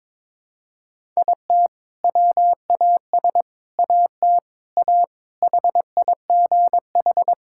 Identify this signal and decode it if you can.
Morse code, heard as a keyed tone, about 22 wpm, 700 Hz. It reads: IT WAS AT A HIGH
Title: Sentences from Top 1000 Words - Review Part 5 - 22wpm